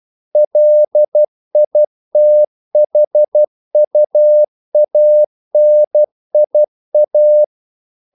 Morse code, heard as a keyed tone, about 12 wpm, 600 Hz. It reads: LITHUANIA
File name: CW / Practice / NationalCapitals12